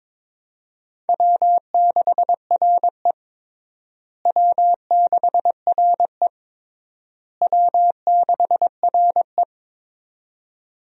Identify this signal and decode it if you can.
Morse code, heard as a keyed tone, about 22 wpm, 700 Hz. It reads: W6RE W6RE W6RE
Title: USA Call Signs - 4 Characters - Repeated 3x - 22wpm